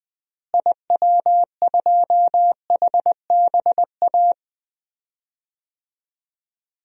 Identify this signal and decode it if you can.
Morse code, heard as a keyed tone, about 20 wpm, 700 Hz. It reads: IW2HBA
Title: DXCC-100 Call Signs - 6 Characters - Repeated 3x - 20wpm